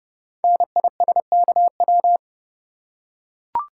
Morse code, heard as a keyed tone, about 30 wpm, 700 Hz. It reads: NISKW
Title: Code Groups - 5 Letters - 30wpm